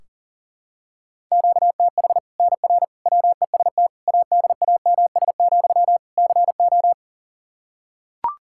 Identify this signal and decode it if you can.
Morse code, heard as a keyed tone, about 40 wpm, 700 Hz. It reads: QTH NR WEST ADAMS, CO
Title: Practice QSO - Exchange 1 - Beginning - 40wpm